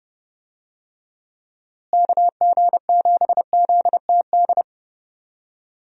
Morse code, heard as a keyed tone, about 30 wpm, 700 Hz. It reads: KG7ZTD
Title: USA Call Signs - 6 Characters - Repeated 3x - 30wpm